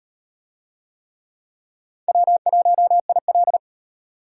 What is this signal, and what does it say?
Morse code, heard as a keyed tone, about 36 wpm, 700 Hz. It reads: W1IL